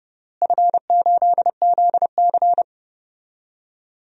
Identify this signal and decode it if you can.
Morse code, heard as a keyed tone, about 31 wpm, 700 Hz. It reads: F8ZC
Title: DXCC-100 Call Signs - 4 Characters - Repeated 3x - 30wpm